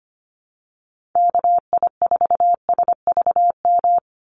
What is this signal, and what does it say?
Morse code, heard as a keyed tone, about 25 wpm, 700 Hz. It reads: KI4SVM
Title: NA SOTA Activator Callsigns 25wpm